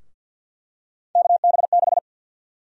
Morse code, heard as a keyed tone, about 46 wpm, 700 Hz. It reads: KB6
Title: USA Call Sign Prefixes - 50wpm